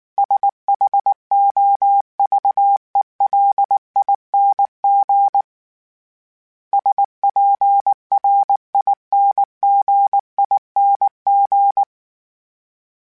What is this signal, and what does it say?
Morse code, heard as a keyed tone, about 19 wpm, 800 Hz. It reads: SHOVELING SPRINGING
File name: Special INGs / Special INGs-mp3s / Sngl Ing @18